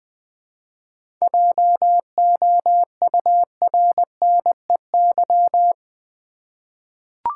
Morse code, heard as a keyed tone, about 21 wpm, 700 Hz. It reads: JOURNEY
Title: 7 Letter Words - First 3 Letters Given - 20wpm